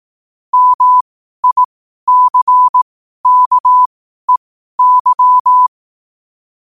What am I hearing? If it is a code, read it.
Morse code, heard as a keyed tone, about 18 wpm, spaced as for 12 wpm, 1000 Hz. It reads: MICKEY
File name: qotdfiles / 12wpm